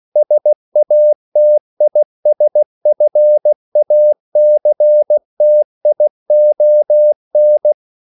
Morse code, heard as a keyed tone, about 16 wpm, 600 Hz. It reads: SATISFACTION